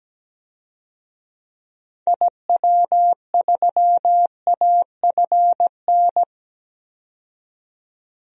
Morse code, heard as a keyed tone, about 17 wpm, 700 Hz. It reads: IW3AFN
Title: DXCC-100 Call Signs - 6 Characters - 17wpm